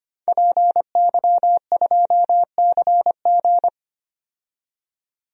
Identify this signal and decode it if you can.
Morse code, heard as a keyed tone, about 25 wpm, 700 Hz. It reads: PY2CG
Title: DXCC-100 Call Signs - 5 Characters - 25wpm